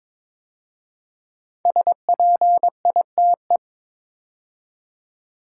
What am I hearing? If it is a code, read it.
Morse code, heard as a keyed tone, about 23 wpm, 700 Hz. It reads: SPITE
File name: 5 Letter Words - First 3 Letters Given - 22wpm